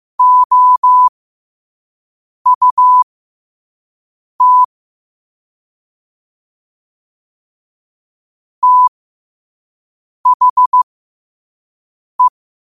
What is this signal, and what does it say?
Morse code, heard as a keyed tone, about 15 wpm, spaced as for 5 wpm, 1000 Hz. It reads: OUT THE
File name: qotdfiles / 5wpm